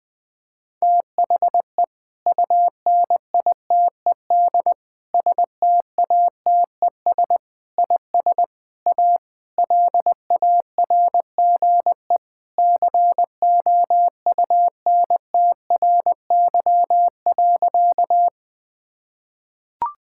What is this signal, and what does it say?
Morse code, heard as a keyed tone, about 20 wpm, 700 Hz. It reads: THE UNITED STATES IS A LARGE COUNTRY.
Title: Sentences with 4-Word Phrases - 20wpm